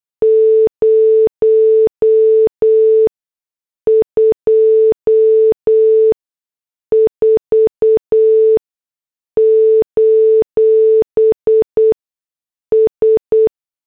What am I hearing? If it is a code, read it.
Morse code, heard as a keyed tone, about 8 wpm, spaced as for 6 wpm, 440 Hz. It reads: 024: S